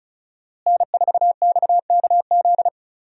Morse code, heard as a keyed tone, about 36 wpm, 700 Hz. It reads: N4XKZ
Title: USA Call Signs - 5 Characters - Repeated 2x - 35wpm